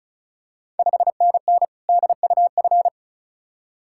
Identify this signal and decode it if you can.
Morse code, heard as a keyed tone, about 36 wpm, 690 Hz. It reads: 5NN DUF
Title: Ontario QSO Party Exchange 35wpm